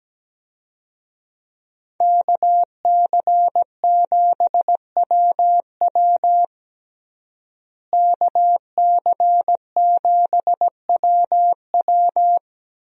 Morse code, heard as a keyed tone, about 17 wpm, 700 Hz. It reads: KC7WW KC7WW
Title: USA Call Signs - 5 Characters - Repeated 3x - 17wpm